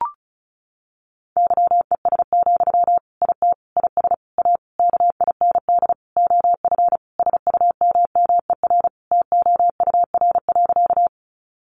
Morse code, heard as a keyed tone, about 34 wpm, 700 Hz. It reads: YES, IT IS A KIND OF SUMMER TOUR.
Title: Sentences from Top 1400 Words 35wpm